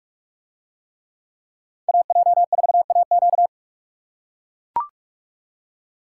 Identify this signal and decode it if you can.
Morse code, heard as a keyed tone, about 44 wpm, 700 Hz. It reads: AJ4AQ